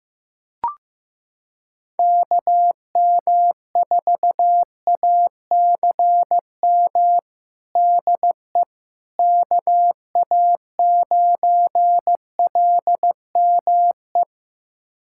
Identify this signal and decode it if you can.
Morse code, heard as a keyed tone, about 15 wpm, 700 Hz. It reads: KM4ACM DE KA9LME
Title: USA Callsign DE Callsign 15wpm